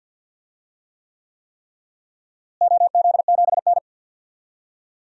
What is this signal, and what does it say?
Morse code, heard as a keyed tone, about 51 wpm, 700 Hz. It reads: OZ7N
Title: DXCC-100 Call Signs - 4 Characters - Repeated 3x - 50wpm